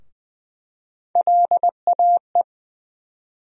Morse code, heard as a keyed tone, about 20 wpm, 700 Hz. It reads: LAE
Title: USA Call Sign Postfixes - 3 Letters 20wpm